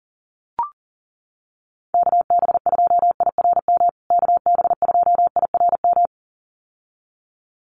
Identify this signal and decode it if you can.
Morse code, heard as a keyed tone, about 40 wpm, 700 Hz. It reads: KB2IRM KB2IRM
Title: USA Call Signs Repeated 2x 40wpm